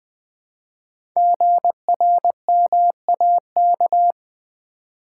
Morse code, heard as a keyed tone, about 20 wpm, 700 Hz. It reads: GRMAK
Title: Code Groups - 5 Letters - 20wpm